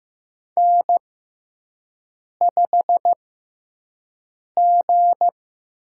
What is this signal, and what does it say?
Morse code, heard as a keyed tone, about 15 wpm, spaced as for 5 wpm, 700 Hz. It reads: N5G
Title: USA Call Signs - 4 Characters - 6x Character Spacing - 15wpm